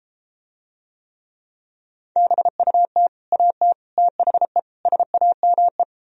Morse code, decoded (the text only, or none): BUT AT THE SAME